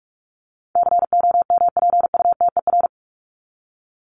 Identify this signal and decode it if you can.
Morse code, heard as a keyed tone, about 44 wpm, 700 Hz. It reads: COMPUTER
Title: Top 501-750 Words 45wpm